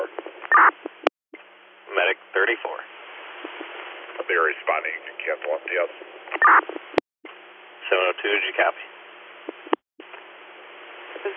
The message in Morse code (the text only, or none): none